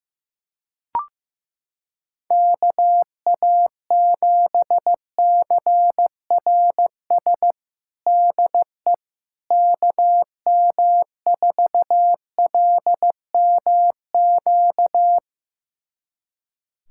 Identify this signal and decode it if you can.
Morse code, heard as a keyed tone, about 15 wpm, 700 Hz. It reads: KA7CRS DE KM4LMQ
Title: USA Callsign DE Callsign 15wpm